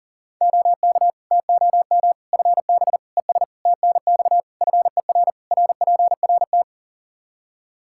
Morse code, heard as a keyed tone, about 40 wpm, 700 Hz. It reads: OK TOM FB ES TNX FER RPRT
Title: Practice QSO - Exchange 2 - Info - 40wpm